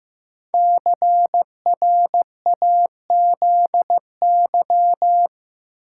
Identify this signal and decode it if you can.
Morse code, heard as a keyed tone, about 15 wpm, 700 Hz. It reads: CRAZY
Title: Five Letter Words - 15wpm